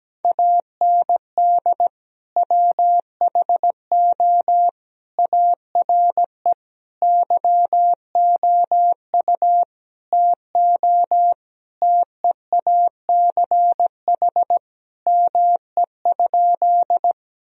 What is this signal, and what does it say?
Morse code, heard as a keyed tone, about 17 wpm, 700 Hz. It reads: AND WHO ARE YOU TO TEACH ME?
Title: Sentences from Top 1000 Words - Review Part 2 - 17wpm